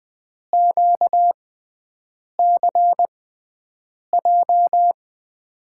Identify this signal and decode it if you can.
Morse code, heard as a keyed tone, about 21 wpm, spaced as for 7 wpm, 700 Hz. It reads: QCJ
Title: USA Call Signs - 6x Character Spacing - 20wpm